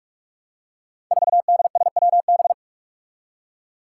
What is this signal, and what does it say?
Morse code, heard as a keyed tone, about 42 wpm, 700 Hz. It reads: 4DSWB
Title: Five Challenging Letter-Number Combinations 45wpm